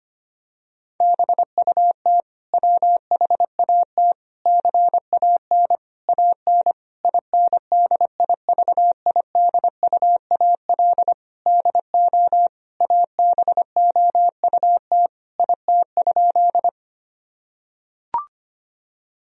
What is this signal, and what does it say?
Morse code, heard as a keyed tone, about 25 wpm, 700 Hz. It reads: BUT WHAT CAN AN INDIVIDUAL DO ABOUT IT?
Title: Sentences from Top 1000 Words - Review Part 4 - 25wpm